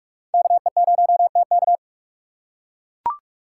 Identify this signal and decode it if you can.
Morse code, heard as a keyed tone, about 42 wpm, 700 Hz. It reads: KE0TK
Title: USA Call Signs - 5 Characters - 6x Character Spacing - 45wpm